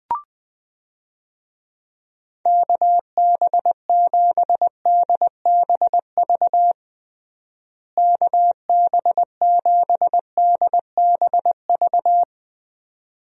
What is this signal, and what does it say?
Morse code, heard as a keyed tone, about 20 wpm, 700 Hz. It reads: KB7DBV KB7DBV
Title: USA Call Signs - 6 Characters - Repeated 2x - 20wpm